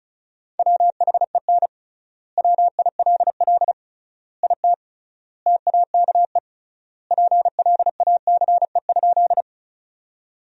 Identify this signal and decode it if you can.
Morse code, heard as a keyed tone, about 36 wpm, 700 Hz. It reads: WHEN WILL IT TAKE PLACE?
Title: 3x Word-Spacing - Sentences from Top 200 Words - 35wpm